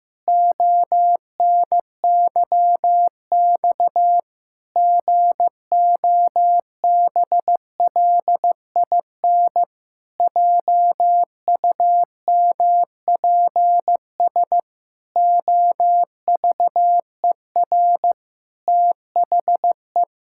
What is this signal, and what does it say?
Morse code, heard as a keyed tone, about 15 wpm, 700 Hz. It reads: ONYX GOBLIN JUMPS OVER THE
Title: Pangram Challenge v2 - 15wpm